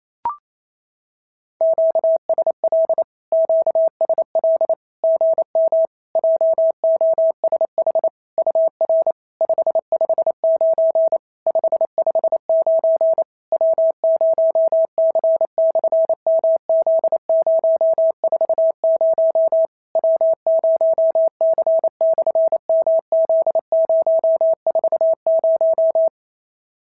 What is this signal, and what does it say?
Morse code, heard as a keyed tone, about 28 wpm, 650 Hz. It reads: QSL QSL GM JOSH UR 559 559 W0C/MZ040 W0C/MZ040
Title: SOTA QSO Practice 28wpm